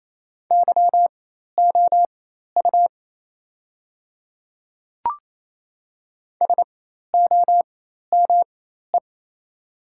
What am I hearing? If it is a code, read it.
Morse code, heard as a keyed tone, about 28 wpm, spaced as for 13 wpm, 700 Hz. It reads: YOU SOME
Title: 4x Character Spacing - Top 100 Words - 28wpm